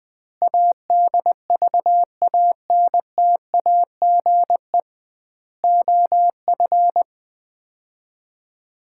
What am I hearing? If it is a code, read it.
Morse code, heard as a keyed tone, about 21 wpm, 700 Hz. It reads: ADVANTAGE OF
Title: 2x Word-Spacing - Sets of 4 Words - 20wpm